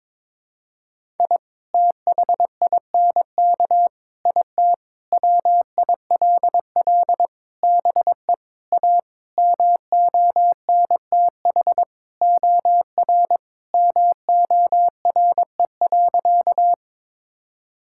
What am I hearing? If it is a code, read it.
Morse code, heard as a keyed tone, about 22 wpm, 700 Hz. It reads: I THINK IT WILL BE A MONTH OR MORE.
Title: Sentences from Top 1000 Words - Review Part 5 - 22wpm